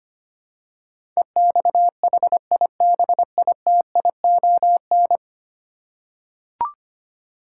Morse code, heard as a keyed tone, about 25 wpm, 700 Hz. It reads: EXHIBITION
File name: 10 Letter Words - First 3 Letters Given - 25wpm